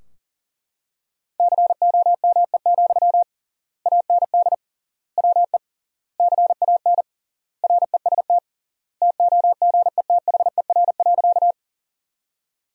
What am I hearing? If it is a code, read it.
Morse code, heard as a keyed tone, about 40 wpm, 700 Hz. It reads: COME, AND WE CAN REST TOGETHER.
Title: 3x Word-Spacing - Sentences from Top 500 Words - 40wpm